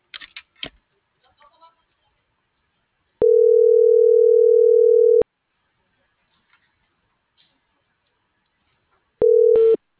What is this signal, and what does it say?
Ringback tone (North America), 440 + 480 Hz.
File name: CENTRAL BLN / out ximena.diaz